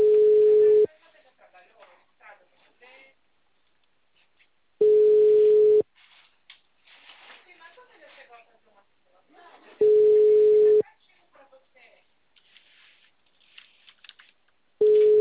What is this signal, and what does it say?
Ringback tone (Europe, 425 Hz), with speech.
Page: Barueri - SP